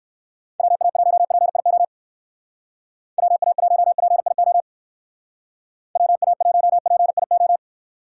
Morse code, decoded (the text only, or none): WA1PIO WA1PIO WA1PIO